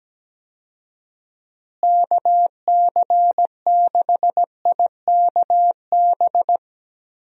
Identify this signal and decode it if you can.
Morse code, heard as a keyed tone, about 17 wpm, 700 Hz. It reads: KC6IKB
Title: USA Call Signs - 6 Characters - Repeated 3x - 17wpm